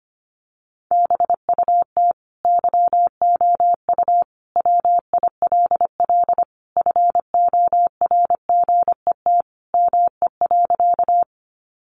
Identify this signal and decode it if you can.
Morse code, heard as a keyed tone, about 25 wpm, 700 Hz. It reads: BUT YOU WILL FORGET ME.
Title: Sentences from Top 1200 Words 25wpm